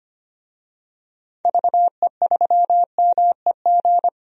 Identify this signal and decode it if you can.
Morse code, heard as a keyed tone, about 25 wpm, 700 Hz. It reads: VE3MEG
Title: DXCC-100 Call Signs - 6 Characters - 25wpm